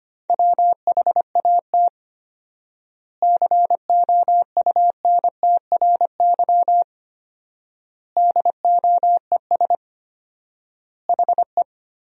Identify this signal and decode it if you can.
Morse code, heard as a keyed tone, about 25 wpm, 700 Hz. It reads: WHAT COUNTRY DOES HE
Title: 4x Word-Spacing - Sentences from Top 300 Words - 25wpm